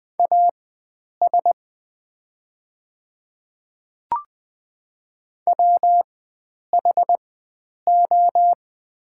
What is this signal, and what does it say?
Morse code, heard as a keyed tone, about 20 wpm, spaced as for 9 wpm, 700 Hz. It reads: AS WHO
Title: 4x Character Spacing - Top 100 Words - 20wpm